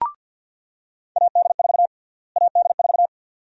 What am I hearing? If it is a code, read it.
Morse code, heard as a keyed tone, about 48 wpm, 700 Hz. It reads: AD4 AD4
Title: USA Call Sign Prefixes - Repeated 2x - 50wpm